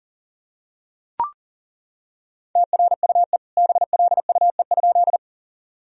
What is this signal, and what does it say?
Morse code, heard as a keyed tone, about 40 wpm, 700 Hz. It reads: TRUE BLUE?
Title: Sentences from Top 1000 Words - Review Part 3 - 40wpm